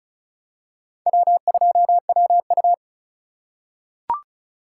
Morse code, heard as a keyed tone, about 36 wpm, 700 Hz. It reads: W2WU
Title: USA Call Signs - 4x Character Spacing - 35wpm